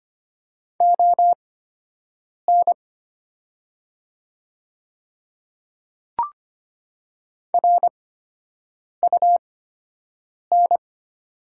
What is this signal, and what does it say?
Morse code, heard as a keyed tone, about 25 wpm, spaced as for 7 wpm, 700 Hz. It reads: ON RUN
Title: 8x Character Spacing - Top 300 Words - 25wpm